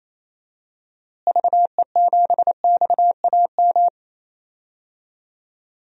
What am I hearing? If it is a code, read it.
Morse code, heard as a keyed tone, about 28 wpm, 700 Hz. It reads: VE7XAM
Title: DXCC-100 Call Signs - 6 Characters - Repeated 3x - 28wpm